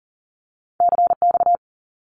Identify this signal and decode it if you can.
Morse code, heard as a keyed tone, about 42 wpm, 700 Hz. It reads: CX
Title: Abbreviations 40wpm